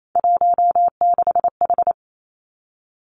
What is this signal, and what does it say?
Morse code, heard as a keyed tone, about 28 wpm, 690 Hz. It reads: 16H
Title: Three Challenging Letter-Number Combinations 28wpm